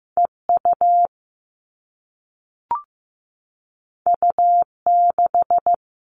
Morse code, heard as a keyed tone, about 15 wpm, 700 Hz. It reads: EU U6
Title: All Cut Numbers - 2-Digit 15wpm